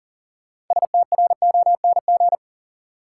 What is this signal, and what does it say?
Morse code, heard as a keyed tone, about 40 wpm, 700 Hz. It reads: STRONG
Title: Sentences from Top 500 Words - New Word Drill - 40wpm